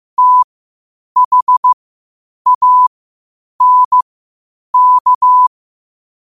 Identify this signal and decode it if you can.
Morse code, heard as a keyed tone, about 15 wpm, spaced as for 8 wpm, 1000 Hz. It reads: THANK
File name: 8wpm